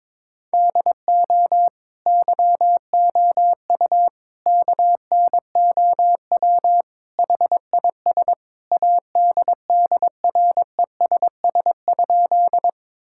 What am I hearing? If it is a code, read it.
Morse code, heard as a keyed tone, about 23 wpm, 700 Hz. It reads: DO YOU KNOW HIS ADDRESS?
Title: Sentences from Top 1000 Words - Review Part 6 - 22wpm